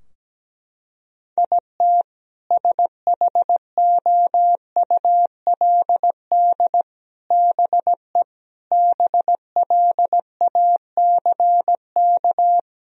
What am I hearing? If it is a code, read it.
Morse code, heard as a keyed tone, about 17 wpm, 700 Hz. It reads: IT SHOULD BE BLACK
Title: Sentences from Top 1000 Words - Review Part 2 - 17wpm